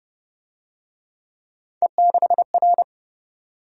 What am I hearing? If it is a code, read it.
Morse code, heard as a keyed tone, about 32 wpm, 700 Hz. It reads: E6R